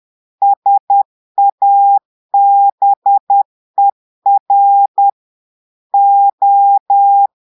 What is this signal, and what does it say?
Morse code, heard as a keyed tone, about 10 wpm, 800 Hz. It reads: SABER O
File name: portugais texte en clair 2 vitesse 10 mots minutes